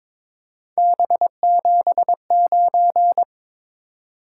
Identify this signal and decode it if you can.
Morse code, heard as a keyed tone, about 26 wpm, 700 Hz. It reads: B79
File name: Speed-Racing - Three Letter-Number Combinations - 22wpm